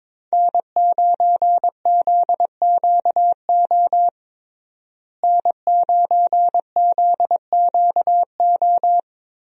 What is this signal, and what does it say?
Morse code, heard as a keyed tone, about 22 wpm, 700 Hz. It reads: N9ZQO N9ZQO
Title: USA Call Signs - 5 Characters - Repeated 2x - 22wpm